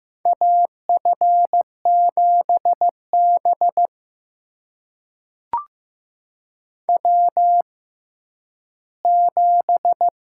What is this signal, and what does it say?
Morse code, heard as a keyed tone, about 15 wpm, spaced as for 10 wpm, 700 Hz. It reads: AF7B W7
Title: USA Call Signs - 4 Characters - 6x Character Spacing - 15wpm